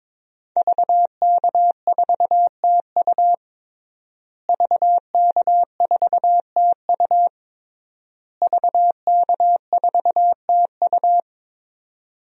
Morse code, heard as a keyed tone, about 22 wpm, 700 Hz. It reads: VK4TU VK4TU VK4TU
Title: DXCC-100 Call Signs - 5 Characters - Repeated 3x - 22wpm